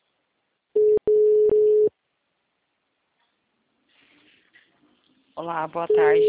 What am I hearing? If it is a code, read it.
Ringback tone (Europe, 425 Hz), with speech.
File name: out- -2008